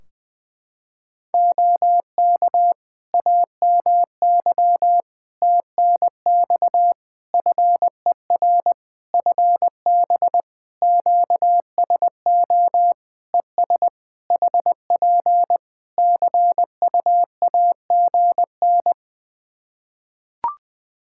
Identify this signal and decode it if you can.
Morse code, heard as a keyed tone, about 20 wpm, 700 Hz. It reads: OK AMY TNX FER FB QSO ES HP CUAGN
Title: Practice QSO - Exchange 4 - Ending - 20wpm